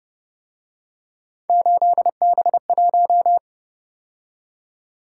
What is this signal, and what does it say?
Morse code, heard as a keyed tone, about 29 wpm, 700 Hz. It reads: 8B1